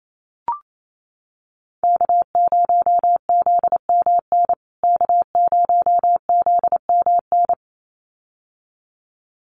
Morse code, heard as a keyed tone, about 28 wpm, 700 Hz. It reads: K0ZMN K0ZMN
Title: USA Call Signs Repeated 2x 28wpm